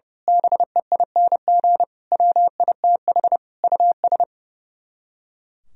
Morse code, heard as a keyed tone, about 28 wpm, 700 Hz. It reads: BEING WITH US
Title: Speed-Racing - Sets of 5 Words 30wpm